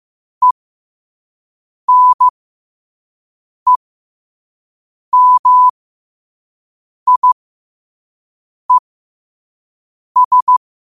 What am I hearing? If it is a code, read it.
Morse code, heard as a keyed tone, about 15 wpm, spaced as for 5 wpm, 1000 Hz. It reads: ENEMIES
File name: qotdfiles / 5wpm